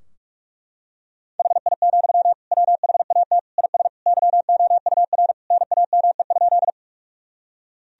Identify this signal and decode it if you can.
Morse code, heard as a keyed tone, about 43 wpm, 700 Hz. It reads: HI, WHAT IS YOUR NAME?
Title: Sentences from Top 1000 Words - Review Part 3 - 45wpm